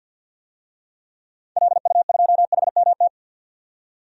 Morse code, heard as a keyed tone, about 45 wpm, 700 Hz. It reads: LU2HMT
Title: DXCC-100 Call Signs - 6 Characters - 50wpm